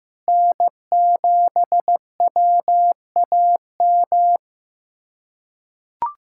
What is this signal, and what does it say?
Morse code, heard as a keyed tone, about 15 wpm, 700 Hz. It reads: N7WAM
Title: USA Callsign DE Callsign - 5 Characters - 15wpm